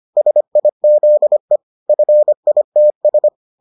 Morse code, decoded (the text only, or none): SIZE FITS